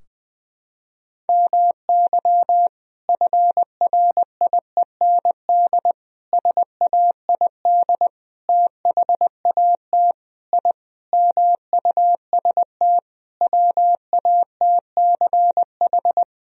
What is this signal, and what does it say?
Morse code, heard as a keyed tone, about 21 wpm, 700 Hz. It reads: MY FRIEND SAID THAT I MUST WATCH
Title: Sentences from Top 1000 Words - Review Part 4 - 20wpm